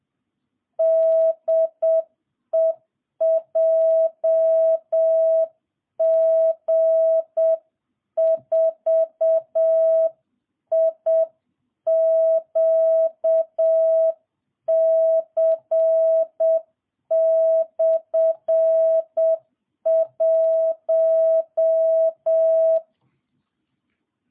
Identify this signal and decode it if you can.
Morse code, heard as a keyed tone, about 7 wpm, 650 Hz. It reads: DE JG4IQC/1